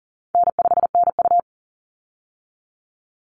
Morse code, heard as a keyed tone, about 41 wpm, 700 Hz. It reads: N5NU
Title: Easy Call Signs - 40wpm